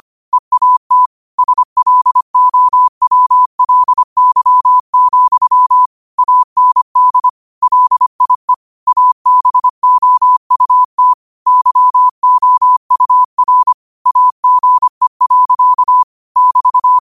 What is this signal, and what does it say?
Morse code, heard as a keyed tone, about 25 wpm, 1000 Hz. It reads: EAT SLOWLY, AND LIE ABOUT YOUR AGE. <BT>